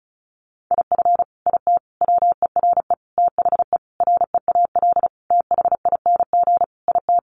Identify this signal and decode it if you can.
Morse code, heard as a keyed tone, about 34 wpm, 700 Hz. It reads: IF IT WERE THE REAL THING IT
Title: Sentences from Top 1300 Words 35wpm